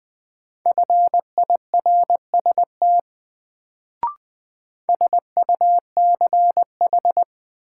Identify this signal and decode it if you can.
Morse code, heard as a keyed tone, about 20 wpm, 700 Hz. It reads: FIRST SUCH
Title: Sets of 2 Words - New Word Drill - 20wpm